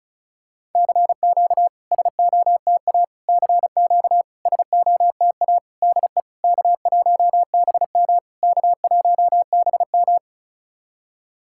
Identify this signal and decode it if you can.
Morse code, heard as a keyed tone, about 34 wpm, 690 Hz. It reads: CQ SOTA CQ SOTA DE K1BM K1BM
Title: SOTA QSO Practice 35wpm